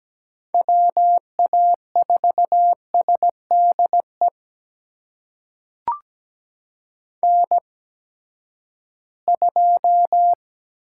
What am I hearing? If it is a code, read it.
Morse code, heard as a keyed tone, about 17 wpm, spaced as for 12 wpm, 700 Hz. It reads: WA4SDE N2
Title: USA Call Signs - 8x Character Spacing - 17wpm